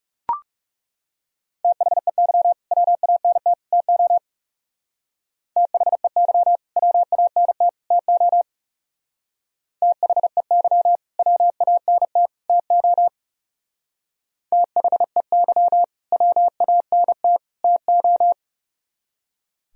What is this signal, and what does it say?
Morse code, heard as a keyed tone, about 37 wpm, 700 Hz. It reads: THEY WANT TO THEY WANT TO THEY WANT TO THEY WANT TO
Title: Speed-Racing - Sets of 3 Words 30wpm